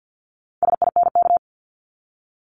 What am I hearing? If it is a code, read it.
Morse code, heard as a keyed tone, about 50 wpm, 700 Hz. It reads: SINK
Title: Multi-letter Phonemes 50wpm